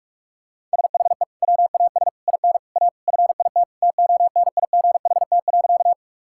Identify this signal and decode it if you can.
Morse code, heard as a keyed tone, about 47 wpm, 700 Hz. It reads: SHE WAS IN A FIT TONIGHT.
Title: Sentences from Top 1000 Words - Review Part 5 - 45wpm